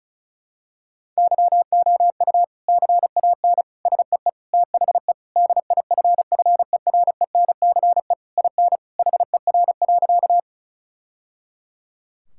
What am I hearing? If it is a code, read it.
Morse code, heard as a keyed tone, about 36 wpm, 700 Hz. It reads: YOU CAN SEE THE DIFFERENCE IN HER.
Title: Sentences from Top 1000 Words - Review Part 5 - 35wpm